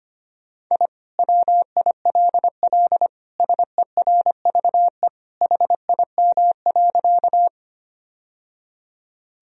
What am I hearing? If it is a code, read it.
Morse code, heard as a keyed tone, about 25 wpm, 700 Hz. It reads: I WILL SERVE HIM.
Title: Sentences from Top 1000 Words - Review Part 2 - 25wpm